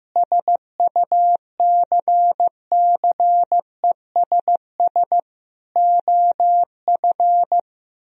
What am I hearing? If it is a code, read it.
Morse code, heard as a keyed tone, about 15 wpm, 700 Hz. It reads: SUCCESS OF
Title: Sentences from Top 1000 Words - Review Part 4 - 15wpm